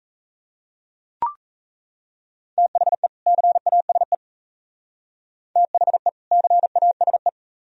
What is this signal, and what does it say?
Morse code, heard as a keyed tone, about 32 wpm, 700 Hz. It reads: THE CASE THE CASE
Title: Speed-Racing - Sets of 2 Words 28wpm